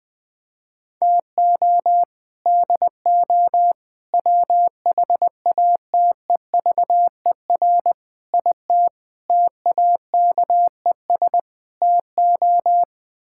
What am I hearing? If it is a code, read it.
Morse code, heard as a keyed tone, about 21 wpm, 700 Hz. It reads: TO DO WHATEVER IT TAKES TO
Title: Sets of 6 Words - 20wpm